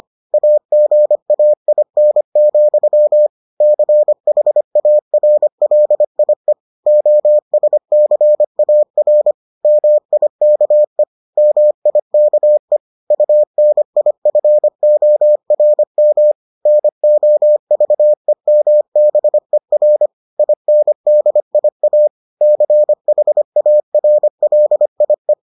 Morse code, heard as a keyed tone, about 25 wpm, 600 Hz. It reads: AGAIN, CHARLIE OSCAR MIKE MIKE UNIFORM NOVEMBER INDIA CHARLIE